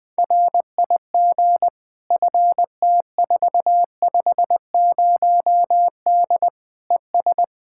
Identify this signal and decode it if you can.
Morse code, heard as a keyed tone, about 20 wpm, 700 Hz. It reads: RIG FT450D ES
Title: Practice QSO - Exchange 2 - Info - 20wpm